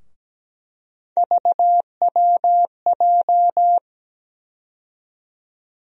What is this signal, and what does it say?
Morse code, heard as a keyed tone, about 17 wpm, 700 Hz. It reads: VWJ
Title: USA Call Sign Postfixes - 3 Letters 17wpm